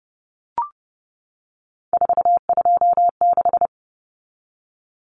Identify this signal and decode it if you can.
Morse code, heard as a keyed tone, about 30 wpm, 700 Hz. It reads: 426
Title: Common Cut Numbers - 3-Digit 30wpm